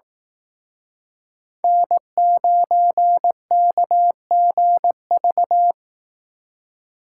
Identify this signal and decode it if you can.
Morse code, heard as a keyed tone, about 18 wpm, 700 Hz. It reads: N9KGV
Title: Speed Racing - USA 1x3 Call Signs 15wpm